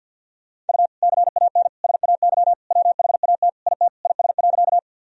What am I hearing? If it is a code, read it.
Morse code, heard as a keyed tone, about 49 wpm, 700 Hz. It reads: U CAN SAY WHAT IT IS.
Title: Sentences from Top 200 Words and Abbreviations 50wpm